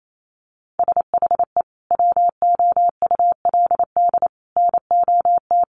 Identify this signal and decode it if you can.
Morse code, heard as a keyed tone, about 28 wpm, 700 Hz. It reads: SHE WOULD NOT
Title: Sentences from Top 1300 Words 28wpm